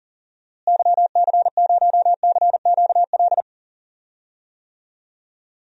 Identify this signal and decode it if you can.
Morse code, heard as a keyed tone, about 39 wpm, 700 Hz. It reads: YC0CQL
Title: DXCC-100 Call Signs - 6 Characters - Repeated 3x - 40wpm